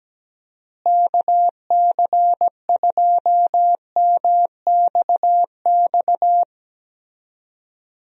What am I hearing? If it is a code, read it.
Morse code, heard as a keyed tone, about 17 wpm, 700 Hz. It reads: KC2MXX
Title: USA Call Signs - 6 Characters - Repeated 3x - 17wpm